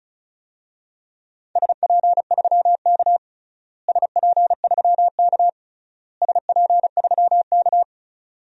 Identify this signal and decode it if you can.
Morse code, heard as a keyed tone, about 36 wpm, 700 Hz. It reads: SP3K SP3K SP3K
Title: DXCC-100 Call Signs - 4 Characters - Repeated 3x - 35wpm